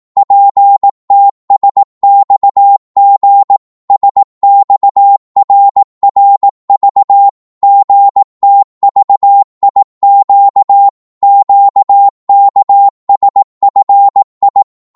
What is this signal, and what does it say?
Morse code, heard as a keyed tone, about 18 wpm, 800 Hz. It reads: PTSXG SXRRV GTVIQ QKSFI
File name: Morse / ltr-18wpm